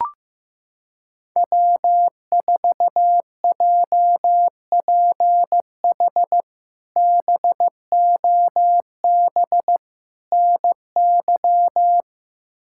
Morse code, heard as a keyed tone, about 15 wpm, 700 Hz. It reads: W4JPH BOB NY
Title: K1USN SST QSO Practice - 15wpm